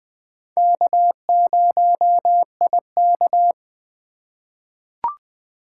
Morse code, heard as a keyed tone, about 21 wpm, 700 Hz. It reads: K0IK
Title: USA Call Signs - 4x Character Spacing - 20wpm